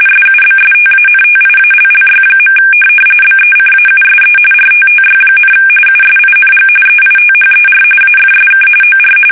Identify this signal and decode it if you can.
RTTY, about 45.45 baud. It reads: SUBSTANTIALLY. BECAUSE RTTY, USING EITHER AFSK OR
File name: fsk850 / teletype-RTTY